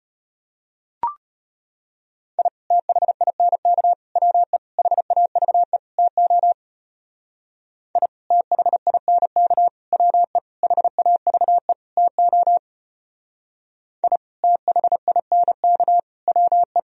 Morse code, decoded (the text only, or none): I THINK WE HAVE TO I THINK WE HAVE TO I THINK WE